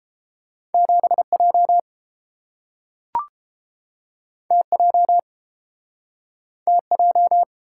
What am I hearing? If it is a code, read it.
Morse code, heard as a keyed tone, about 30 wpm, 700 Hz. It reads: 7J TJ TJ